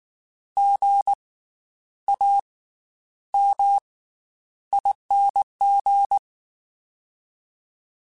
Morse code, heard as a keyed tone, about 19 wpm, 800 Hz. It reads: G A M ING
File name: Special INGs / Special INGs-mp3s / Slow ing @18